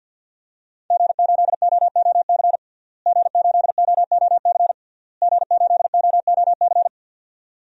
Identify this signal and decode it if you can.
Morse code, heard as a keyed tone, about 51 wpm, 700 Hz. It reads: G8OOC G8OOC G8OOC
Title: DXCC-100 Call Signs - 5 Characters - Repeated 3x - 50wpm